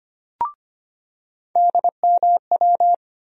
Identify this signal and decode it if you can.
Morse code, heard as a keyed tone, about 25 wpm, 700 Hz. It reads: DMW
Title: USA Call Sign Postfixes - 3 Letters 25wpm